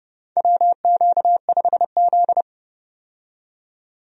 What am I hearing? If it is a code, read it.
Morse code, heard as a keyed tone, about 30 wpm, 700 Hz. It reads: WQ5Z
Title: USA Call Signs - 4 Characters - 30wpm